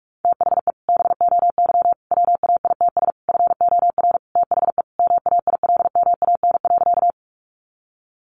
Morse code, heard as a keyed tone, about 44 wpm, 700 Hz. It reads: THE BOY WAITS FOR THE MAILMAN.
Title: Additional Easy Phrases 45wpm